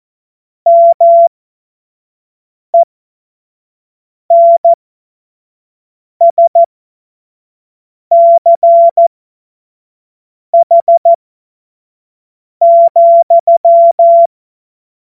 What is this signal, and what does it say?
Morse code, heard as a keyed tone, about 14 wpm, spaced as for 5 wpm, 680 Hz. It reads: MENSCH,
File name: 25-F70